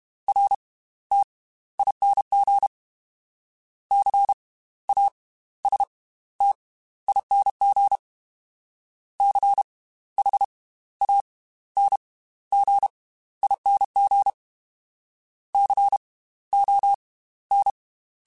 Morse code, heard as a keyed tone, about 32 wpm, spaced as for 16 wpm, 800 Hz. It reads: RTING CASTING CHANGING CON
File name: Slow ing @30